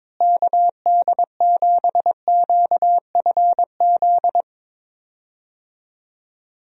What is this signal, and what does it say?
Morse code, heard as a keyed tone, about 22 wpm, 700 Hz. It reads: KD7QFZ
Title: USA Call Signs - 6 Characters - Repeated 3x - 22wpm